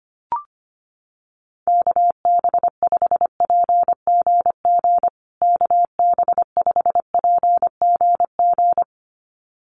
Morse code, heard as a keyed tone, about 25 wpm, 700 Hz. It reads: KB5PGG KB5PGG
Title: USA Call Signs Repeated 2x 25wpm